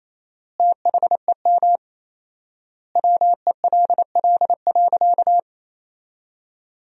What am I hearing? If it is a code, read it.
Morse code, heard as a keyed tone, about 28 wpm, 700 Hz. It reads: THEM WELL.
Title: 4x Word-Spacing - Sentences from Top 300 Words - 28wpm